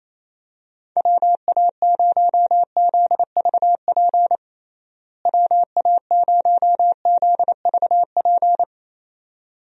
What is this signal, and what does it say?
Morse code, heard as a keyed tone, about 28 wpm, 700 Hz. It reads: WA0ZVP WA0ZVP
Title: USA Call Signs - 6 Characters - Repeated 2x - 28wpm